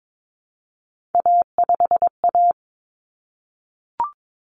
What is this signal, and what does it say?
Morse code, heard as a keyed tone, about 22 wpm, 700 Hz. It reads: A5A
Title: Common Cut Numbers - 3-Digit 22wpm